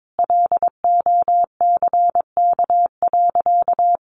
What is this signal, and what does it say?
Morse code, heard as a keyed tone, about 22 wpm, 700 Hz. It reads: LOCK.
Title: Sentences from Top 1300 Words 22wpm